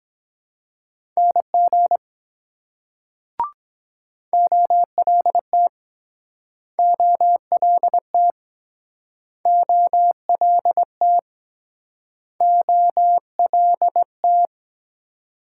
Morse code, heard as a keyed tone, about 24 wpm, 700 Hz. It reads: NG OLT OLT OLT OLT